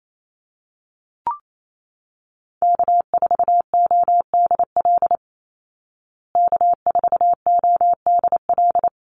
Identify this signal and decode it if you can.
Morse code, heard as a keyed tone, about 28 wpm, 700 Hz. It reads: K4ODL K4ODL
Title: Call Signs Repeated 3x - 28wpm